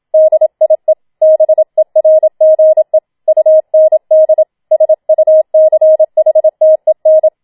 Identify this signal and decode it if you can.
Morse code, heard as a keyed tone, about 27 wpm, 630 Hz. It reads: DIE BERGE UND SUCHTEN
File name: Schneewittchen Teil2 27wpm